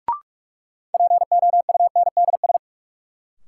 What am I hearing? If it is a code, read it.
Morse code, heard as a keyed tone, about 43 wpm, 700 Hz. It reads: POUNDS
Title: Sentences from Top 1000 Words - Review Part 2 - New Word Drill - 45wpm